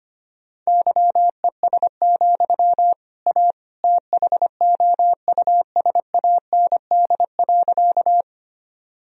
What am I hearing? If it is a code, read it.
Morse code, heard as a keyed tone, about 25 wpm, 700 Hz. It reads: YES, A THOUSAND.
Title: Sentences from Top 1000 Words - Review Part 3 - 25wpm